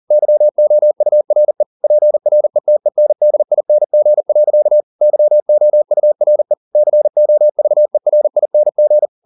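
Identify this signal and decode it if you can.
Morse code, heard as a keyed tone, about 40 wpm, 600 Hz. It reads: YOURE PRETENDING. YOURE COVERING
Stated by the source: Morse code at 40 wpm, practice text, CW med Gnister.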